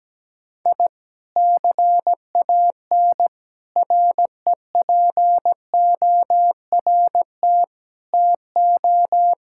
Morse code, heard as a keyed tone, about 17 wpm, 700 Hz. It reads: I CAN REPORT TO
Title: Sentences from Top 1000 Words - Review Part 6 - 17wpm